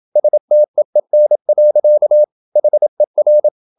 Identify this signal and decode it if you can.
Morse code, heard as a keyed tone, about 27 wpm, 600 Hz. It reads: STEEN. HER
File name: Ferie QRQ 27wpm